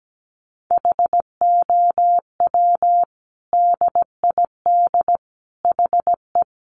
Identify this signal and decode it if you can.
Morse code, heard as a keyed tone, about 17 wpm, 700 Hz. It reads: HOW DID HE
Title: Sentences from Top 1400 Words 17wpm